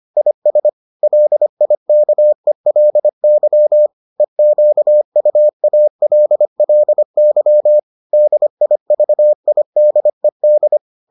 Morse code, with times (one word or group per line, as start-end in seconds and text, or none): 0.2-0.7 IS
1.0-3.9 LIKELY
4.2-7.8 EQUALLY
8.1-10.8 DIVIDED